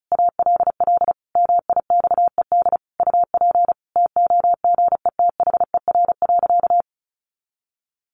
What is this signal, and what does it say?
Morse code, heard as a keyed tone, about 34 wpm, 700 Hz. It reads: ALL MIXED UP TOGETHER.
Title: Sentences from Top 1500 Words 35wpm